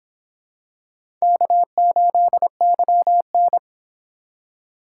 Morse code, heard as a keyed tone, about 25 wpm, 700 Hz. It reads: K8YN
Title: Speed Racing - USA Call Signs 22wpm